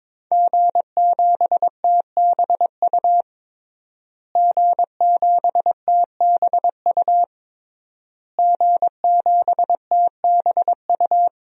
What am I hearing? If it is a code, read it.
Morse code, heard as a keyed tone, about 22 wpm, 700 Hz. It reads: G7TBU G7TBU G7TBU
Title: DXCC-100 Call Signs - 5 Characters - Repeated 3x - 22wpm